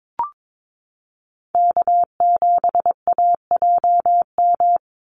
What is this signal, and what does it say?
Morse code, heard as a keyed tone, about 22 wpm, 700 Hz. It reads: K7AJM
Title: Call Signs - 22wpm